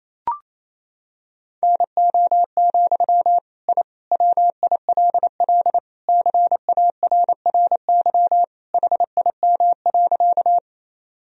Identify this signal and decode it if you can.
Morse code, heard as a keyed tone, about 28 wpm, 700 Hz. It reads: NO, I WILL CARRY HIM.
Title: Sentences from Top 1000 Words - Review Part 1 - 28wpm